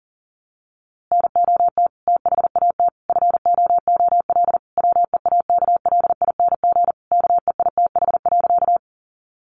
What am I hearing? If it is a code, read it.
Morse code, heard as a keyed tone, about 40 wpm, 700 Hz. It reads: NOT THAT FOOL WEAKLING KEITH.
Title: A Sound Of Thunder 40wpm